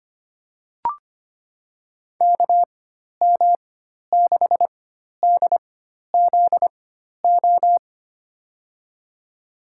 Morse code, heard as a keyed tone, about 25 wpm, spaced as for 12 wpm, 700 Hz. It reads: KM6DZO
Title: USA Call Signs - 6 Characters - 4x Character Spacing - 25wpm